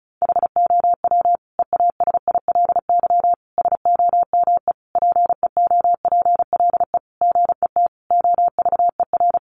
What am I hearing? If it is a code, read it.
Morse code, heard as a keyed tone, about 34 wpm, 700 Hz. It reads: HOW EASILY SOME PEOPLE GET OVER
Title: Sentences from Top 900 Words 35wpm